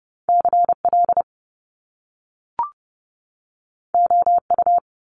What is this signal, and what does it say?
Morse code, heard as a keyed tone, about 30 wpm, 700 Hz. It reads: CL OU
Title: Top Two Letter Combinations in Words 30wpm